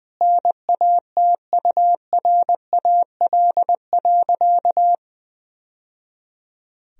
Morse code, heard as a keyed tone, about 20 wpm, 700 Hz. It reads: NATURAL.
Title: Sentences from Top 1000 Words - Review Part 4 - 20wpm